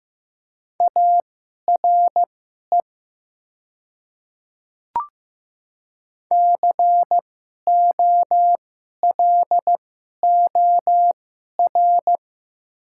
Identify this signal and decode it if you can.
Morse code, heard as a keyed tone, about 15 wpm, spaced as for 11 wpm, 700 Hz. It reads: ARE COLOR
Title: 2x Character Spacing - Top 300 Words - 15wpm